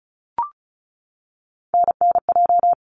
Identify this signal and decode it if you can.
Morse code, heard as a keyed tone, about 36 wpm, 700 Hz. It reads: NNJ
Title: ARRL Sections 35wpm